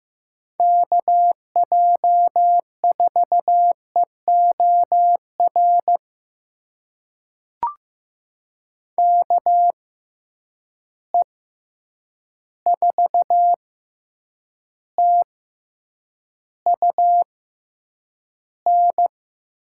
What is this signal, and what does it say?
Morse code, heard as a keyed tone, about 15 wpm, spaced as for 8 wpm, 700 Hz. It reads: KJ4EOR KE4TUN
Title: USA Call Signs - 6 Characters - 6x Character Spacing - 15wpm